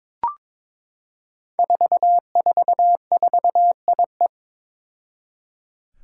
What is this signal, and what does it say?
Morse code, heard as a keyed tone, about 22 wpm, 700 Hz. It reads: 444IE
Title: Five Challenging Letter-Number Combinations 22wpm